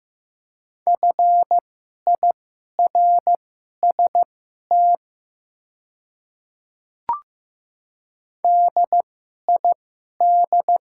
Morse code, heard as a keyed tone, about 15 wpm, spaced as for 11 wpm, 700 Hz. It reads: FIRST DID
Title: 2x Character Spacing - Top 100 Words - 15wpm